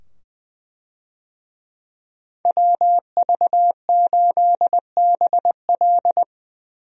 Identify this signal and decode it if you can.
Morse code, heard as a keyed tone, about 20 wpm, 700 Hz. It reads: WV8BL
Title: USA Call Signs - 5 Characters - 20wpm